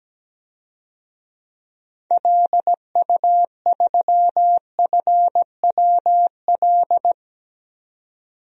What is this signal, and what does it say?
Morse code, heard as a keyed tone, about 17 wpm, 700 Hz. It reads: LU3FWL
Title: DXCC-100 Call Signs - 6 Characters - 17wpm